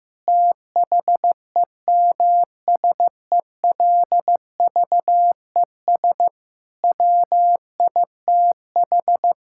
Morse code, heard as a keyed tone, about 16 wpm, 700 Hz. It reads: THEMSELVES WITH
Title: Sentences from Top 1000 Words - Review Part 5 - 15wpm